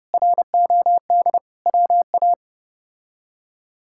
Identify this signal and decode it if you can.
Morse code, heard as a keyed tone, about 32 wpm, 700 Hz. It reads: ROD WA
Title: K1USN SST QSO Practice - 30wpm